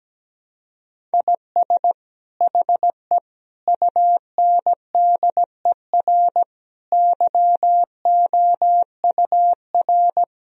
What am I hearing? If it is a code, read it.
Morse code, heard as a keyed tone, about 17 wpm, 700 Hz. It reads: IS HE UNDER YOUR
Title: Sentences from Top 1000 Words - Review Part 1 - 17wpm